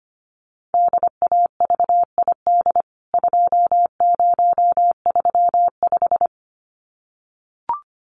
Morse code, heard as a keyed tone, about 25 wpm, 700 Hz. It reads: DAVID 2035
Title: CWT Exchange 25wpm